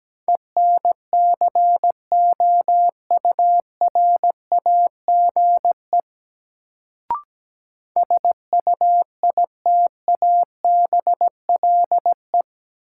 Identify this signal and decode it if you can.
Morse code, heard as a keyed tone, about 17 wpm, 700 Hz. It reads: ENCOURAGE SUITABLE
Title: Sentences from Top 1400 Words - New Word Drill - 17wpm